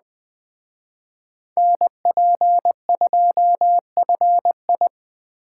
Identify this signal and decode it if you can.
Morse code, heard as a keyed tone, about 19 wpm, 700 Hz. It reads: NP2FI